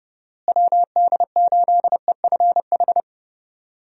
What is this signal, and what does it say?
Morse code, heard as a keyed tone, about 30 wpm, 700 Hz. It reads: WD8EFH
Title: USA Call Signs - 6 Characters - Repeated 3x - 30wpm